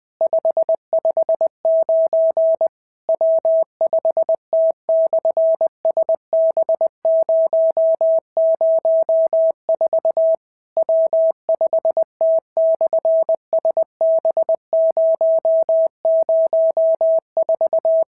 Morse code, heard as a keyed tone, about 19 wpm, 650 Hz. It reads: 559 W5T/SB004 W5T/SB004
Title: SOTA QSO Practice 20wpm